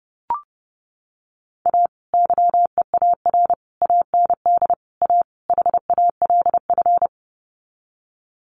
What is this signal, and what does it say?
Morse code, heard as a keyed tone, about 30 wpm, 700 Hz. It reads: A YEAR AND A HALF
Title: Sets of 5 Words 30wpm